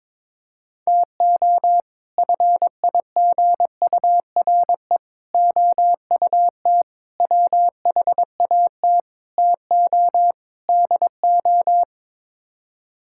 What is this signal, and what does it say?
Morse code, heard as a keyed tone, about 22 wpm, 700 Hz. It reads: TO FIGURE OUT WHAT TO DO
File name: practice / 22wpm / Sets of 6 Words - 22wpm